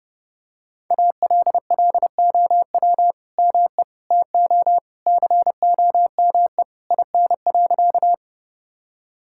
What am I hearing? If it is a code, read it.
Morse code, heard as a keyed tone, about 30 wpm, 700 Hz. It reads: ALLOW ME TO COME IN.
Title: Sentences from Top 1000 Words - Review Part 5 - 30wpm